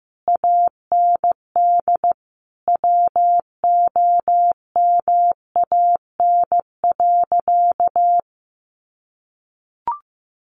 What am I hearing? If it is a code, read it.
Morse code, heard as a keyed tone, about 15 wpm, 700 Hz. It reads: AND WOMAN.
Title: Sentences from Top 1100 Words 15wpm